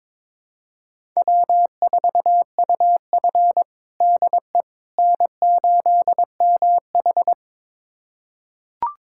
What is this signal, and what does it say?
Morse code, heard as a keyed tone, about 22 wpm, 700 Hz. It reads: W4UF DE N8MH
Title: USA Callsign DE Callsign - 4 Characters - 22wpm